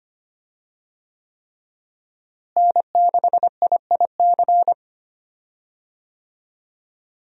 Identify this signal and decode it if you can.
Morse code, heard as a keyed tone, about 25 wpm, 700 Hz. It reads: N6IIC
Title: USA Call Signs - 5 Characters - Repeated 3x - 25wpm